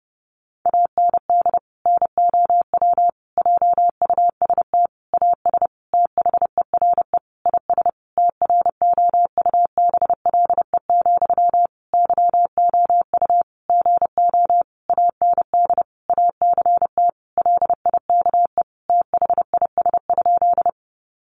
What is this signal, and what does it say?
Morse code, heard as a keyed tone, about 30 wpm, 700 Hz. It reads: AND NOW JUST AS THERE IS TROUBLE, YOU GO AND ACT LIKE THIS?
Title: Sentences from Top 1200 Words 30wpm